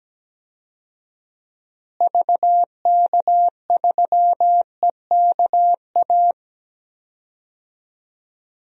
Morse code, heard as a keyed tone, about 17 wpm, 700 Hz. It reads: VK3EKA
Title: DXCC-100 Call Signs - 6 Characters - Repeated 2x - 17wpm